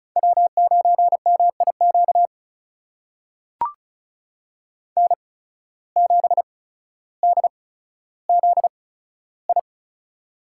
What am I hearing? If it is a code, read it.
Morse code, heard as a keyed tone, about 36 wpm, spaced as for 15 wpm, 700 Hz. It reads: W9MIQ N7DZI
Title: USA Call Signs - 5 Characters - 8x Character Spacing - 35wpm